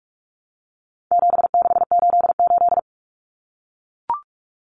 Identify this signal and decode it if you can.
Morse code, heard as a keyed tone, about 44 wpm, 700 Hz. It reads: 7688
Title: Common Cut Numbers - 4-Digit 45wpm